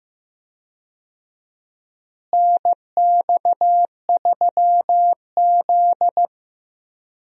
Morse code, heard as a keyed tone, about 15 wpm, 700 Hz. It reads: NX3Z
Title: USA Call Signs - 4 Characters - Repeated 3x - 15wpm